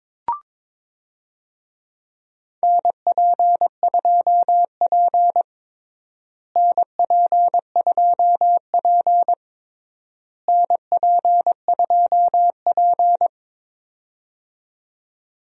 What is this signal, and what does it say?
Morse code, heard as a keyed tone, about 22 wpm, 700 Hz. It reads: NP2P NP2P NP2P
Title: USA Call Signs - 4 Characters - Repeated 3x - 22wpm